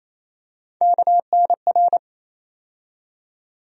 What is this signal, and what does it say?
Morse code, heard as a keyed tone, about 28 wpm, 700 Hz. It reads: KNR
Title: USA Call Sign Postfixes - 3 Letters 28wpm